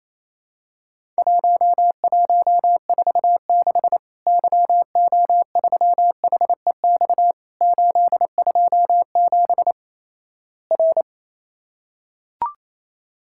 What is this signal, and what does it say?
Morse code, heard as a keyed tone, about 28 wpm, 700 Hz. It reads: 1146 YO3HEX 827 R
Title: Worked All Europe DX Contest - QTC Exchange - 28wpm